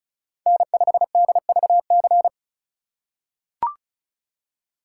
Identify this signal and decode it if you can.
Morse code, heard as a keyed tone, about 36 wpm, 700 Hz. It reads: N5DVC
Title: USA Call Signs - 2x Character Spacing - 35wpm